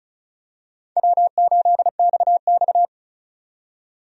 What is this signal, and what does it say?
Morse code, heard as a keyed tone, about 34 wpm, 700 Hz. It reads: W8XX
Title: USA Call Signs - 4 Characters - 35wpm